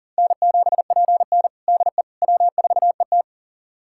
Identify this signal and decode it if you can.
Morse code, heard as a keyed tone, about 37 wpm, 700 Hz. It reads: N7PN DE W4ET
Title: USA Callsign DE Callsign - 4 Characters - 40wpm